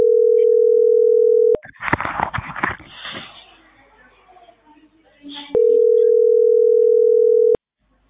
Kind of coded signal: telephone call progress tone